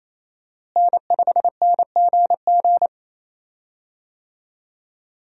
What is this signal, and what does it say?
Morse code, heard as a keyed tone, about 28 wpm, 700 Hz. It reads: N5NGG
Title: USA Call Signs - 5 Characters - Repeated 3x - 28wpm